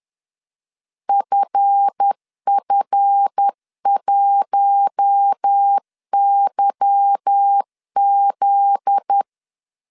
Morse code, heard as a keyed tone, about 10 wpm, 800 Hz. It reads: FF1YZ
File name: LEZIONI / 10 SIGNS / 10C DOT 10wpm 2940